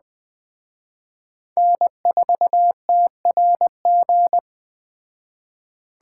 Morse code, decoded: N4TRG

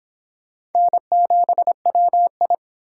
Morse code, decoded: N7WI

